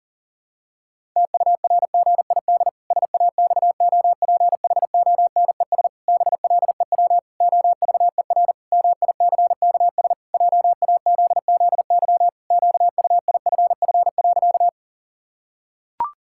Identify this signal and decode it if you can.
Morse code, heard as a keyed tone, about 40 wpm, 700 Hz. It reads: TURGID SAXOPHONES BLEW OVER MICKS JAZZY QUIFF.
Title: Pangram Challenge v2 - 40wpm